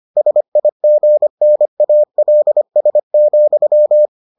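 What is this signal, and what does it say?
Morse code, heard as a keyed tone, about 25 wpm, 600 Hz. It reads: SIGNALS,
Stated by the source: Morse edition of the Foundations of Amateur Radio episodes.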